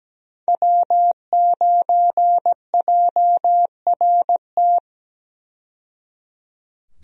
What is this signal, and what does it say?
Morse code, heard as a keyed tone, about 17 wpm, 700 Hz. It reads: W9JRT
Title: USA Call Signs - 5 Characters - 17wpm